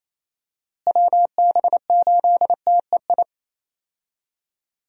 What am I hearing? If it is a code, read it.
Morse code, heard as a keyed tone, about 28 wpm, 700 Hz. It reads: WB8TEI
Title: USA Call Signs - 6 Characters - Repeated 2x - 28wpm